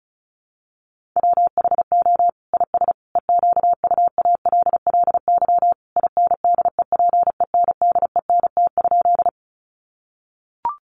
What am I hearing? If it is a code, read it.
Morse code, heard as a keyed tone, about 34 wpm, 700 Hz. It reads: WHO IS EQUALLY INDEPENDENT?
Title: Sentences from Top 1400 Words 35wpm